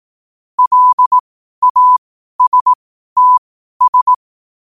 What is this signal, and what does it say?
Morse code, heard as a keyed tone, about 18 wpm, spaced as for 12 wpm, 1000 Hz. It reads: LASTS
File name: qotdfiles / 12wpm